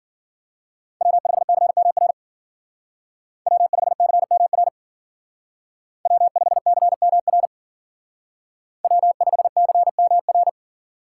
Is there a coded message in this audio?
Yes: Morse code from 1.0 to 10.5 s, heard as a keyed tone, about 52 wpm, 700 Hz. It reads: W5CMR W5CMR W5CMR W5CMR